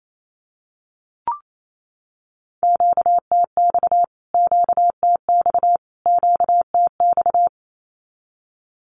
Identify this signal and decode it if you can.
Morse code, heard as a keyed tone, about 28 wpm, 700 Hz. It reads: QTX QTX QTX
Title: Q-Codes Repeated 3x 28wpm